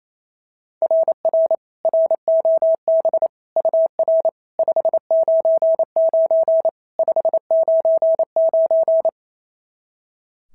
Morse code, heard as a keyed tone, about 25 wpm, 650 Hz. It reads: RR ROB UR 599 599
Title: SOTA QSO Practice 28wpm